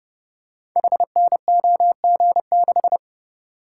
Morse code, heard as a keyed tone, about 30 wpm, 700 Hz. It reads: HNOG6
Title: Code Groups - 5 Letter:Numbers - 30wpm